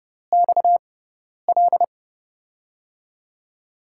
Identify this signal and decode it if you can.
Morse code, heard as a keyed tone, about 31 wpm, spaced as for 10 wpm, 700 Hz. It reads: XL
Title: USA Call Signs - 6 Characters - 6x Character Spacing - 30wpm